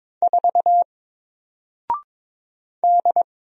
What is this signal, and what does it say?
Morse code, heard as a keyed tone, about 22 wpm, 700 Hz. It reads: 4 D
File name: Single-Letter-Number-22wpm